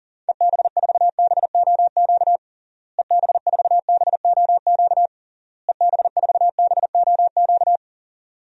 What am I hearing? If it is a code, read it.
Morse code, heard as a keyed tone, about 40 wpm, 700 Hz. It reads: EB4BOQ EB4BOQ EB4BOQ
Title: DXCC-100 Call Signs - 6 Characters - Repeated 3x - 40wpm